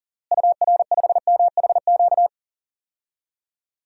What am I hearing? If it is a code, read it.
Morse code, heard as a keyed tone, about 40 wpm, 700 Hz. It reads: UR5MHQ
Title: DXCC-100 Call Signs - 6 Characters - Repeated 3x - 40wpm